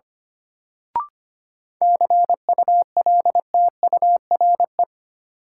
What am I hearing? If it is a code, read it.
Morse code, heard as a keyed tone, about 25 wpm, 700 Hz. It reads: CULTURE
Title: Sentences from Top 1000 Words - Review Part 6 - New Word Drill - 25wpm